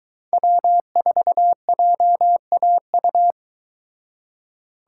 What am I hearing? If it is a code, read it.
Morse code, heard as a keyed tone, about 23 wpm, 700 Hz. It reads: W4JAU